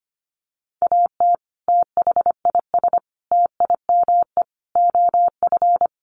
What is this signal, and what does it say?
Morse code, heard as a keyed tone, about 25 wpm, 700 Hz. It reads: AT THIS TIME OF
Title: Sentences from Top 300 Words 25wpm